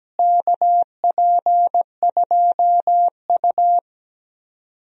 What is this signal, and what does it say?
Morse code, heard as a keyed tone, about 17 wpm, 700 Hz. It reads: KP2U
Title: DXCC-100 Call Signs - 4 Characters - Repeated 2x - 17wpm